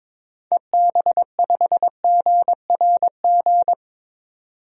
Morse code, heard as a keyed tone, about 22 wpm, 700 Hz. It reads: EB5GRG
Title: DXCC-100 Call Signs - 6 Characters - 22wpm